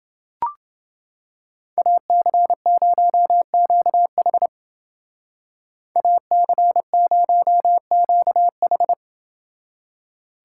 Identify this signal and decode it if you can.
Morse code, heard as a keyed tone, about 29 wpm, 700 Hz. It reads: AC0QH AC0QH